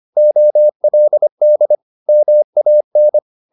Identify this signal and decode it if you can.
Morse code, heard as a keyed tone, about 25 wpm, 600 Hz. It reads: OLD MAN
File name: the old man and the sea part 1 25wpm